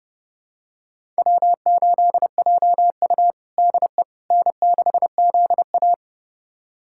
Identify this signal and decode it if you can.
Morse code, heard as a keyed tone, about 30 wpm, 700 Hz. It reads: W8JU DE N6ZA
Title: USA Callsign DE Callsign - 4 Characters - 30wpm